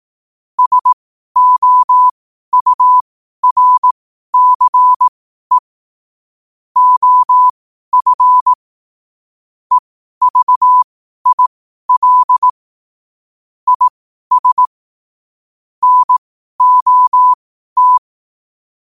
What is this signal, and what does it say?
Morse code, heard as a keyed tone, about 18 wpm, spaced as for 12 wpm, 1000 Hz. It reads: SOURCE OF EVIL IS NOT
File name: qotdfiles / 12wpm